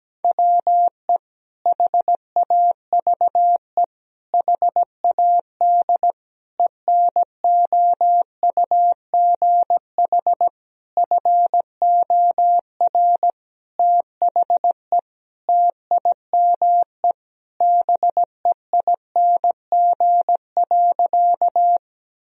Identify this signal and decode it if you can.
Morse code, heard as a keyed tone, about 17 wpm, 700 Hz. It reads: WE HAVE HAD ENOUGH FOR THE TIME BEING.
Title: Sentences from Top 1000 Words - Review Part 3 - 17wpm